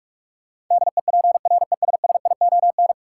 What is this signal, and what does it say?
Morse code, heard as a keyed tone, about 43 wpm, 700 Hz. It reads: DEPRESSION
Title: Sentences from Top 1300 Words - New Word Drill - 45wpm